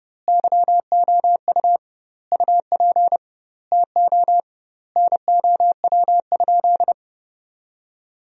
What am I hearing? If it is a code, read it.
Morse code, heard as a keyed tone, about 30 wpm, 700 Hz. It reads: YOU UP TO NOW?
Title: 2x Word-Spacing - Sentences from Top 100 Words - 30wpm